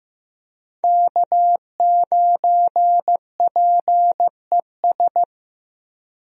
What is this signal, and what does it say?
Morse code, heard as a keyed tone, about 15 wpm, 700 Hz. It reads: K9PES
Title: USA Call Signs - 5 Characters - Repeated 2x - 15wpm